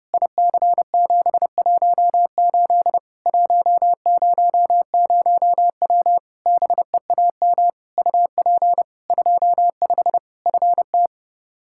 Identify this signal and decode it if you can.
Morse code, heard as a keyed tone, about 30 wpm, 700 Hz. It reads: IC718 100W BEAM UP 25 FT
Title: Practice QSO - Exchange 2 - Info - 30wpm